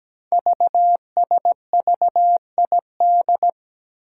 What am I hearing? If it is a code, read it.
Morse code, heard as a keyed tone, about 17 wpm, 700 Hz. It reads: VSVID
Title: Code Groups - 5 Letters - 17wpm